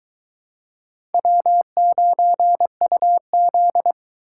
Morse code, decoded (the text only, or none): W9UZ